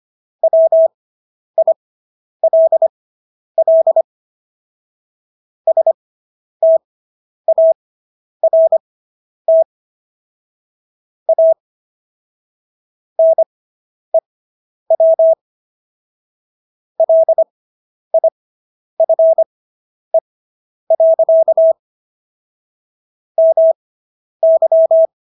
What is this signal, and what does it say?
Morse code, heard as a keyed tone, about 25 wpm, spaced as for 10 wpm, 650 Hz. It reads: WILL START A NEW LIFE. MY